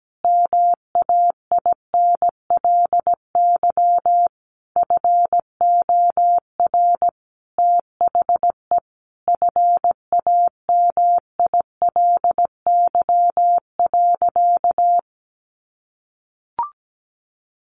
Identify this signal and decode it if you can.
Morse code, heard as a keyed tone, about 17 wpm, 700 Hz. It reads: MAINLY FOR THE FAMILY.
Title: Sentences from Top 1400 Words 17wpm